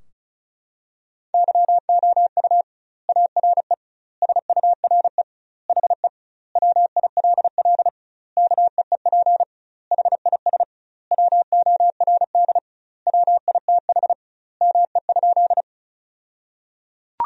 Morse code, heard as a keyed tone, about 34 wpm, 700 Hz. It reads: YOU ARE SURE HE WILL KEEP HIS WORD WITH ME?
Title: 2x Word-Spacing - Sentences from Top 300 Words - 35wpm